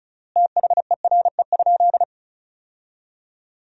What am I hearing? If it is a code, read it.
Morse code, heard as a keyed tone, about 34 wpm, 700 Hz. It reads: THERE?
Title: 3x Word-Spacing - Sentences from Top 500 Words - 35wpm